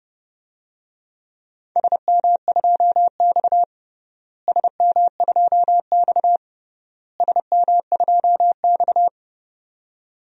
Morse code, heard as a keyed tone, about 30 wpm, 700 Hz. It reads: SM2X SM2X SM2X
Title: DXCC-100 Call Signs - 4 Characters - Repeated 3x - 30wpm